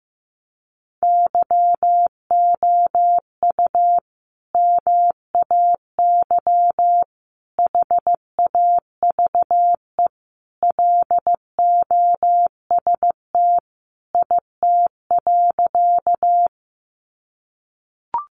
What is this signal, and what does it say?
Morse code, heard as a keyed tone, about 15 wpm, 700 Hz. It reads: YOU MAY HAVE LOST IT.
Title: Sentences from Top 1400 Words 15wpm